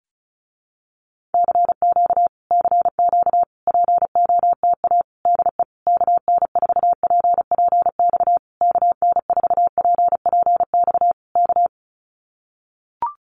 Morse code, heard as a keyed tone, about 34 wpm, 700 Hz. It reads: CQ CQ POTA DE KN4PPX KN4PPX K
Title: Common QSO Phrases Encore 35wpm